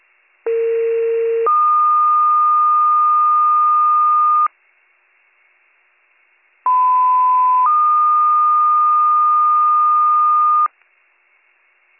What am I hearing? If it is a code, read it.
Two-tone page, heard as a one-second tone, then a three-second tone, about 460 then 1220 Hz, about 1010 then 1220 Hz. It is heard from 0.5 to 10.7 s.